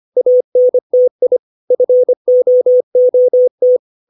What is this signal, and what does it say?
Morse code, heard as a keyed tone, about 25 wpm, 500 Hz. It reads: ANTI FOOT
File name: 632 words / 632 Words 25WPM